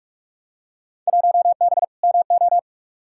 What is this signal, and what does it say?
Morse code, heard as a keyed tone, about 42 wpm, 700 Hz. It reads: 1B MO
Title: Field Day Exchanges - 45wpm